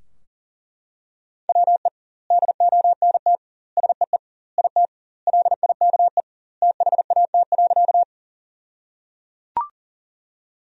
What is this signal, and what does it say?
Morse code, heard as a keyed tone, about 37 wpm, 700 Hz. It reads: WE DONT SEE IT LIKE THAT.
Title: 2x Word-Spacing - Sentences from Top 300 Words - 40wpm